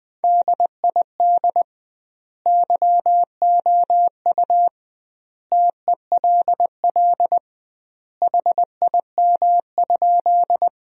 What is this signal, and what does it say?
Morse code, heard as a keyed tone, about 20 wpm, 700 Hz. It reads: DID YOU TELL HIM?
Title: 2x Word-Spacing - Sentences from Top 200 Words - 20wpm